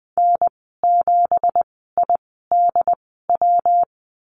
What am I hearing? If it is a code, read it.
Morse code, heard as a keyed tone, about 20 wpm, spaced as for 14 wpm, 700 Hz. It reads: N7IDW
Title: 219 Lesson 35g - Number 0 - Call Signs 20wpm